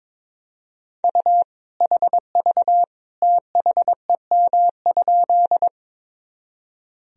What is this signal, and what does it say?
Morse code, heard as a keyed tone, about 22 wpm, 700 Hz. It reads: U HV THEM?
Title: Sentences from Top 100 Words and Abbreviations 22wpm